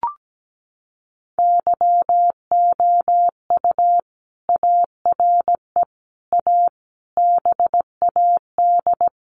Morse code, heard as a keyed tone, about 17 wpm, 700 Hz. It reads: YOU ARE A BAD
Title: Sentences from Top 700 Words 17wpm